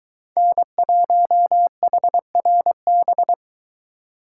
Morse code, heard as a keyed tone, about 23 wpm, 700 Hz. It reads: N1HRB